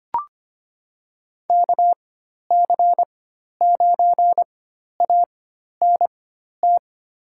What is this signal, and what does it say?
Morse code, heard as a keyed tone, about 25 wpm, spaced as for 12 wpm, 700 Hz. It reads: KC9ANT
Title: USA Call Signs - 6 Characters - 4x Character Spacing - 25wpm